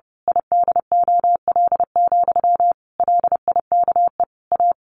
Morse code, heard as a keyed tone, about 28 wpm, 700 Hz. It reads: IDOL, LIKE A
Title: A Sound Of Thunder 30wpm